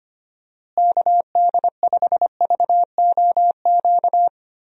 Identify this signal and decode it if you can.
Morse code, heard as a keyed tone, about 25 wpm, 700 Hz. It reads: KD5VOQ
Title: USA Call Signs - 6 Characters - Repeated 2x - 25wpm